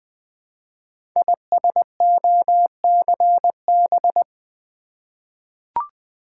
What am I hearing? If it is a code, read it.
Morse code, heard as a keyed tone, about 21 wpm, 700 Hz. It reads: ISOCB
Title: Code Groups - 5 Letters - 20wpm